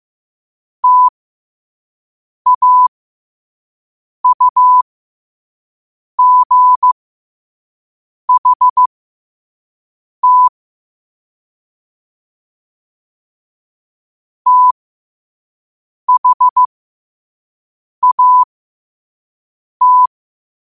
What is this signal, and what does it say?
Morse code, heard as a keyed tone, about 15 wpm, spaced as for 5 wpm, 1000 Hz. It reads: TAUGHT THAT
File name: qotdfiles / 5wpm